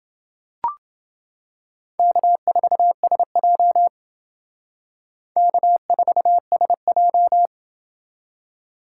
Morse code, heard as a keyed tone, about 28 wpm, 700 Hz. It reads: K4SJ K4SJ